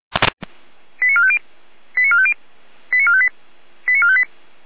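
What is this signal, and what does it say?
Five-tone selective call (ZVEI), heard as quick runs of five tones, tones of about 70 ms. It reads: 89450 89450 89458 89458
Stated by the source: ZVEI five-tone call 89350, received by SDR.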